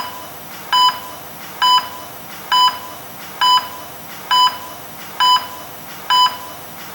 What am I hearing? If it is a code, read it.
There is no code.